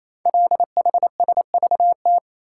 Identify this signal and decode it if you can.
Morse code, heard as a keyed tone, about 27 wpm, 700 Hz. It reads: LHSVT
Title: Code Groups - 5 Letters - 28wpm